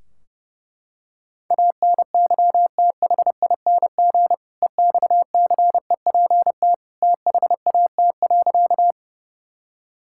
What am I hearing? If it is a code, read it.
Morse code, heard as a keyed tone, about 30 wpm, 700 Hz. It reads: ANYTHING EXCEPT THAT.
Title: Sentences from Top 1600 Words 30wpm